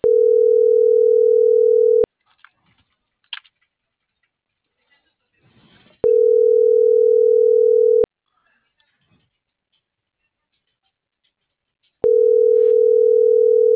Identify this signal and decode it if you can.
Ringback tone (North America), 440 + 480 Hz.
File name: CENTRAL MABE / out luis.palacios - .77591